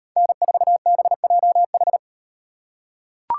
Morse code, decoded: N4BJH

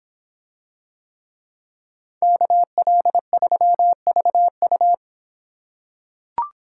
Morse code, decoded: KL3VU

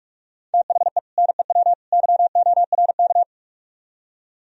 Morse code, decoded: THE NEW YORK

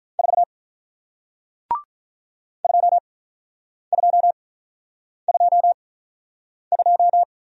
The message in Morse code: 4 2222

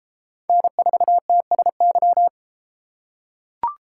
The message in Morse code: N4TSY

